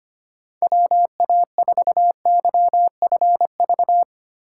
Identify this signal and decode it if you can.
Morse code, heard as a keyed tone, about 25 wpm, 700 Hz. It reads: WA4YFV
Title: USA Call Signs - 6 Characters - 3x Character Spacing - 25wpm